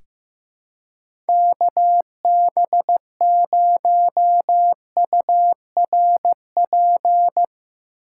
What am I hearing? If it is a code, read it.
Morse code, heard as a keyed tone, about 15 wpm, 700 Hz. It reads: KB0URP
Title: USA Call Signs - 6 Characters - 4x Character Spacing - 15wpm